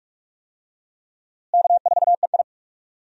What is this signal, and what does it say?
Morse code, heard as a keyed tone, about 41 wpm, 700 Hz. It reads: K4EI
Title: USA Call Signs - 4 Characters - Repeated 3x - 45wpm